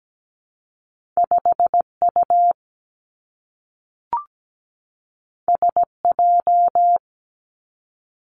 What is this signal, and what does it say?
Morse code, heard as a keyed tone, about 17 wpm, 700 Hz. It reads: 5U SJ
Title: Two Challenging Letter-Number Combinations 17wpm